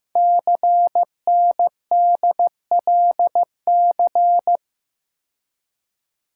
Morse code, heard as a keyed tone, about 15 wpm, 700 Hz. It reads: CNDLC
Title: Code Groups - 5 Letters - 15wpm